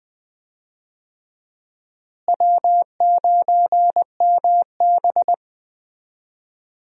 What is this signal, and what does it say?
Morse code, heard as a keyed tone, about 21 wpm, 700 Hz. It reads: W9MB
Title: USA Call Signs - 4 Characters - Repeated 3x - 20wpm